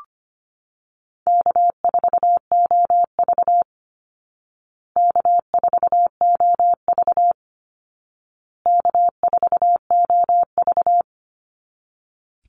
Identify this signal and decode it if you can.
Morse code, heard as a keyed tone, about 25 wpm, 700 Hz. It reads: K4OV K4OV K4OV
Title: Call Signs Repeated 2x - 25wpm